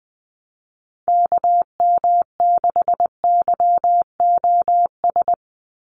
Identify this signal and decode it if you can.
Morse code, heard as a keyed tone, about 20 wpm, 700 Hz. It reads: KM6YOS
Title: USA Call Signs 20wpm